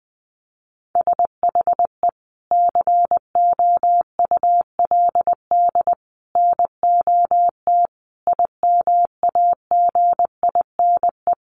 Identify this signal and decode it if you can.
Morse code, heard as a keyed tone, about 20 wpm, 700 Hz. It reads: SHE COULD NOT IMAGINE
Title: Sentences from Top 1400 Words 20wpm